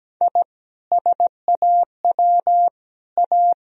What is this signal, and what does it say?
Morse code, heard as a keyed tone, about 17 wpm, 700 Hz. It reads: I SAW A
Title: Sentences from Top 1000 Words - Review Part 2 - 17wpm